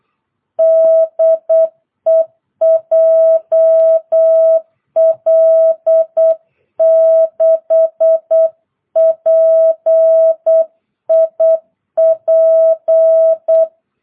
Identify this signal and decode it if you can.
Morse code, heard as a keyed tone, about 8 wpm, 660 Hz. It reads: DE JL6PIP